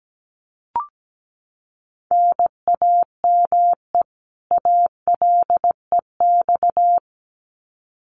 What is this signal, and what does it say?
Morse code, heard as a keyed tone, about 17 wpm, 700 Hz. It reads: NAME ALEX
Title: Common QSO Phrases 17wpm